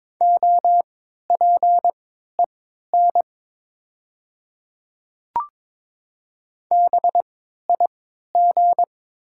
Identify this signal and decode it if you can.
Morse code, heard as a keyed tone, about 22 wpm, spaced as for 12 wpm, 700 Hz. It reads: OPEN BIG
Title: 3x Character Spacing - Top 300 Words - 22wpm